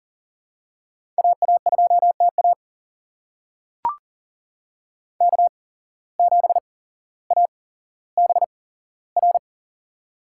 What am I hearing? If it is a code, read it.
Morse code, heard as a keyed tone, about 41 wpm, spaced as for 17 wpm, 700 Hz. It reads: AA2TA K7ABR
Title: USA Call Signs - 5 Characters - 8x Character Spacing - 40wpm